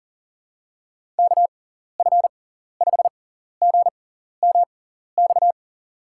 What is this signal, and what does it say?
Morse code, heard as a keyed tone, about 40 wpm, spaced as for 14 wpm, 700 Hz. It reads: KF5GMX
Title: USA Call Signs - 6 Characters - 6x Character Spacing - 40wpm